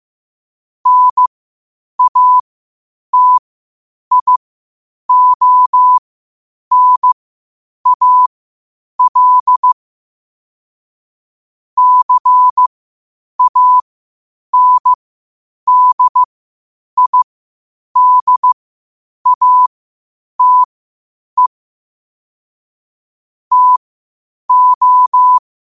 Morse code, heard as a keyed tone, about 15 wpm, spaced as for 8 wpm, 1000 Hz. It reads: NATIONAL CANDIDATE TO
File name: qotdfiles / 8wpm